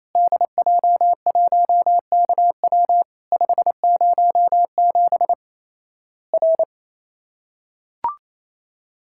Morse code, heard as a keyed tone, about 28 wpm, 700 Hz. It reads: DJ1KW 507 R
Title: Worked All Europe DX Contest - QTC Exchange - 28wpm